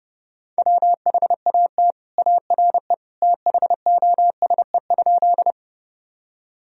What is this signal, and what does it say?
Morse code, heard as a keyed tone, about 30 wpm, 700 Hz. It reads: WHAT ARE THOSE?
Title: Sentences from Top 1000 Words - Review Part 1 - 30wpm